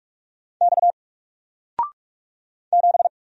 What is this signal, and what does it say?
Morse code, heard as a keyed tone, about 45 wpm, 700 Hz. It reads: X 7
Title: Single Letter-Number 45wpm